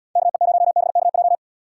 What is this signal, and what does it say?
Morse code, heard as a keyed tone, about 79 wpm, 700 Hz. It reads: KE0BKJ